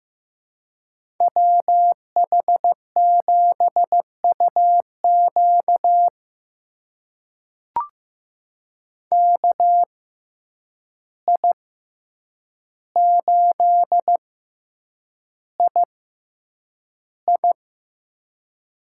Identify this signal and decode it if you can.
Morse code, heard as a keyed tone, about 15 wpm, spaced as for 8 wpm, 700 Hz. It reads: WH7UQ KI8II
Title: USA Call Signs - 5 Characters - 6x Character Spacing - 15wpm